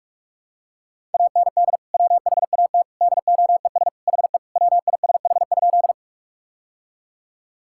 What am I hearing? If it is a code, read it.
Morse code, heard as a keyed tone, about 48 wpm, 700 Hz. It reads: AND WHAT DOES HE WISH?
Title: Sentences from Top 1600 Words 45wpm